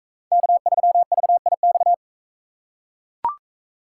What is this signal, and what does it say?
Morse code, heard as a keyed tone, about 42 wpm, 700 Hz. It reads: K3VIX